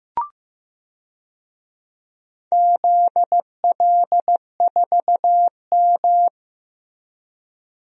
Morse code, heard as a keyed tone, about 15 wpm, 700 Hz. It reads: ZL4M
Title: DXCC-100 Call Signs - 4 Characters - Repeated 2x - 15wpm